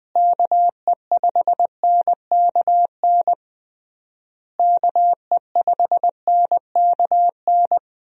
Morse code, heard as a keyed tone, about 20 wpm, 700 Hz. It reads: KE5NKN KE5NKN
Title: USA Call Signs - 6 Characters - Repeated 3x - 20wpm